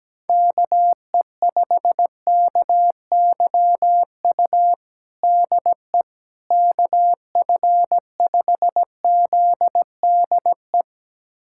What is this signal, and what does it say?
Morse code, heard as a keyed tone, about 17 wpm, 700 Hz. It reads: KE5KYU DE KF5ZDE
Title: USA Callsign DE Callsign 17wpm